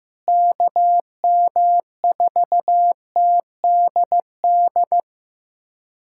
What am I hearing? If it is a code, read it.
Morse code, heard as a keyed tone, about 15 wpm, 700 Hz. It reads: KM4TDD
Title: USA Callsign DE Callsign - 6 Characters - 15wpm